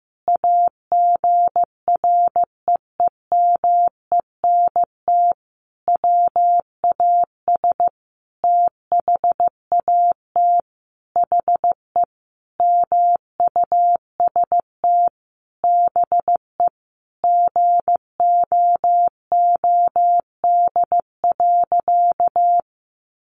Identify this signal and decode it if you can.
Morse code, heard as a keyed tone, about 14 wpm, 700 Hz. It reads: AGREEMENT WAS THAT HE MUST BE GOOD.
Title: Sentences from Top 1500 Words 15wpm